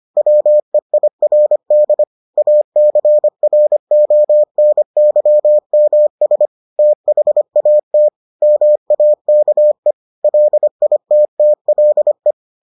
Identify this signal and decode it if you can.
Morse code, heard as a keyed tone, about 25 wpm, 600 Hz. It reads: WEIRD ACRONYMS THAT MAKE LITTLE